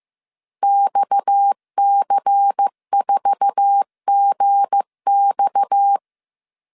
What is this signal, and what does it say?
Morse code, heard as a keyed tone, about 15 wpm, 800 Hz. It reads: XC4GX
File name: CW / DET / 1 9 1500C S A 14wpm 3222